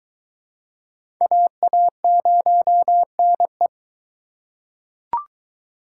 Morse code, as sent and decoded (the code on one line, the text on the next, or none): .- .- ----- -. .
AA0NE